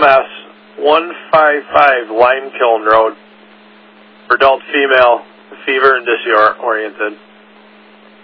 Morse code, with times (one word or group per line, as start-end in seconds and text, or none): none